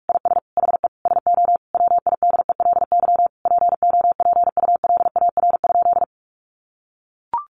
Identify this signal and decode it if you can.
Morse code, heard as a keyed tone, about 44 wpm, 700 Hz. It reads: IS HE SO WIDELY POPULAR?
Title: Sentences from Top 1200 Words 45wpm